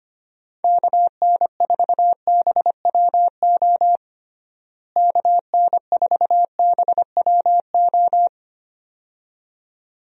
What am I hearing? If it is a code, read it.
Morse code, heard as a keyed tone, about 25 wpm, 700 Hz. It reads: KN4BWO KN4BWO
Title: USA Call Signs - 6 Characters - Repeated 2x - 25wpm